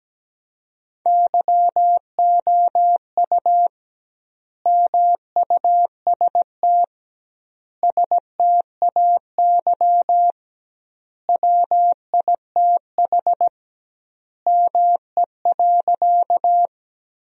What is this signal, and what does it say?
Morse code, heard as a keyed tone, about 17 wpm, 700 Hz. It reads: YOU MUST STAY WITH ME.
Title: 2x Word-Spacing - Sentences from Top 500 Words - 17wpm